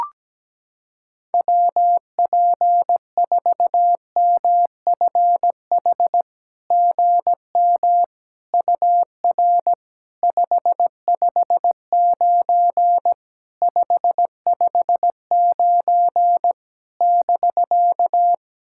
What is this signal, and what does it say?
Morse code, heard as a keyed tone, about 17 wpm, 700 Hz. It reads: WP4MFH GM UR 559 559 <BK>
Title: SOTA QSO Practice 17wpm